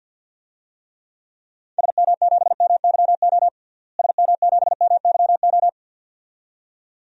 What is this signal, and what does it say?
Morse code, heard as a keyed tone, about 50 wpm, 700 Hz. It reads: SM7MYO SM7MYO
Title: DXCC-100 Call Signs - 6 Characters - Repeated 2x - 50wpm